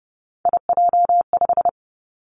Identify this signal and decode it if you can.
Morse code, heard as a keyed tone, about 30 wpm, 700 Hz. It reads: IJ5
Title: Three Challenging Letter-Number Combinations 30wpm